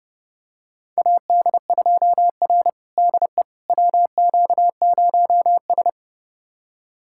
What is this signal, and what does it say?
Morse code, heard as a keyed tone, about 30 wpm, 700 Hz. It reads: AD2R DE WQ0S
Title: USA Callsign DE Callsign - 4 Characters - 30wpm